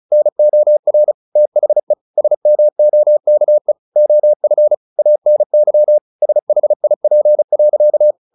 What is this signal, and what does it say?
Morse code, heard as a keyed tone, about 34 wpm, 600 Hz. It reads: NOR THE SMOKE OF ANY SHIP.
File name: the old man and the sea 2 35wpm